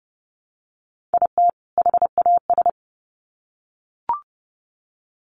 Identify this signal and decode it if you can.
Morse code, heard as a keyed tone, about 30 wpm, 700 Hz. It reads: IT HAS
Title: Sets of 2 Words 30wpm